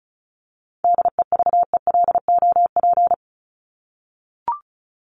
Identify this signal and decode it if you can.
Morse code, heard as a keyed tone, about 34 wpm, 700 Hz. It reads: DEVELOP
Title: ED Suffix - Top 100 Words with Base Words 35wpm